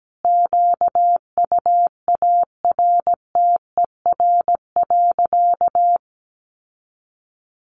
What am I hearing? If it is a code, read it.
Morse code, heard as a keyed tone, about 17 wpm, 700 Hz. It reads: QUARTER.
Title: Sentences from Top 1500 Words 17wpm